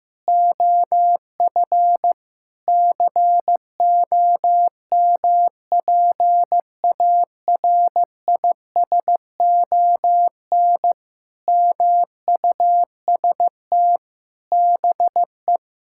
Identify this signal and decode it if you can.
Morse code, heard as a keyed tone, about 15 wpm, 700 Hz. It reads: OF COMPARISON MUST BE
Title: Sentences from Top 1600 Words 15wpm